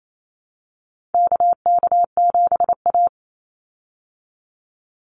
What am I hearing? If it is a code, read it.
Morse code, heard as a keyed tone, about 28 wpm, 700 Hz. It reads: KK7A
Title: Call Signs Repeated 2x - 28wpm